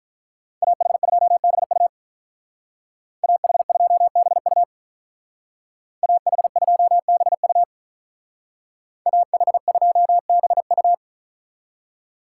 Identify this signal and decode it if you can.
Morse code, heard as a keyed tone, about 42 wpm, 700 Hz. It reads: AH2BU AH2BU AH2BU AH2BU
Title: Speed Racing - USA 2x2 Call Signs 35wpm